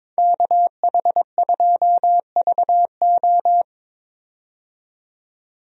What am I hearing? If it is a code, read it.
Morse code, heard as a keyed tone, about 22 wpm, 700 Hz. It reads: KH2VO
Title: USA Call Signs - 5 Characters - 22wpm